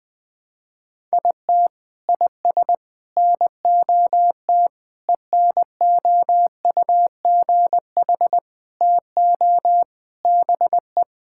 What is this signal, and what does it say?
Morse code, heard as a keyed tone, about 20 wpm, 700 Hz. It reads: IT IS NOT ENOUGH TO BE
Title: Sentences from Top 1000 Words - Review Part 6 - 20wpm